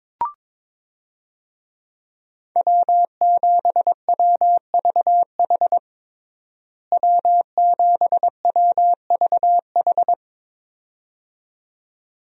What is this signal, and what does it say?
Morse code, heard as a keyed tone, about 22 wpm, 700 Hz. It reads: W7WVH W7WVH
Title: USA Call Signs - 5 Characters - Repeated 2x - 22wpm